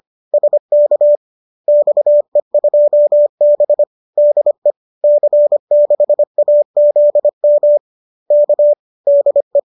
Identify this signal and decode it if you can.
Morse code, heard as a keyed tone, about 25 wpm, 570 Hz. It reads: SK XE2B DE C6AZM K DE
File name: qso204 25